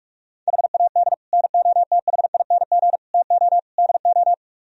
Morse code, heard as a keyed tone, about 49 wpm, 700 Hz. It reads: HAD NOTHING TO DO